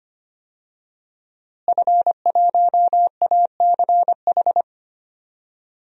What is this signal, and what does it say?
Morse code, heard as a keyed tone, about 25 wpm, 700 Hz. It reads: F1ACH
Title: DXCC-100 Call Signs - 5 Characters - Repeated 2x - 25wpm